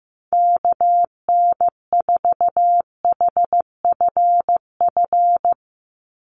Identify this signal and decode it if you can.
Morse code, heard as a keyed tone, about 15 wpm, 700 Hz. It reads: KN4HFF
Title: USA Call Signs 15wpm